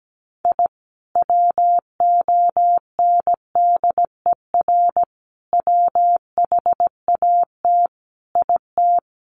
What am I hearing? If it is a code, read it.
Morse code, heard as a keyed tone, about 17 wpm, 700 Hz. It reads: I WONDER WHAT IT
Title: Sentences from Top 600 Words 17wpm